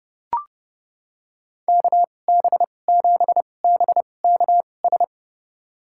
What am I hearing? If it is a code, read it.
Morse code, heard as a keyed tone, about 30 wpm, spaced as for 22 wpm, 700 Hz. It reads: KB7BKS
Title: USA Call Signs - 6 Characters - 2x Character Spacing - 30wpm